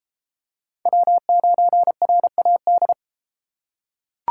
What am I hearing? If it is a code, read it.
Morse code, heard as a keyed tone, about 31 wpm, 700 Hz. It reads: W9RAD